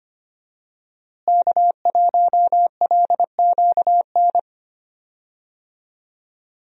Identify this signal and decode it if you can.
Morse code, heard as a keyed tone, about 25 wpm, 700 Hz. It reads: K1LQN
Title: USA Call Signs - 5 Characters - Repeated 2x - 25wpm